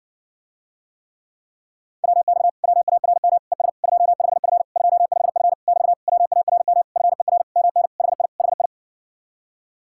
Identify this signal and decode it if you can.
Morse code, heard as a keyed tone, about 60 wpm, 700 Hz. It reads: WX WARM ES 25F 25F <BT> WARM FER MN HI HI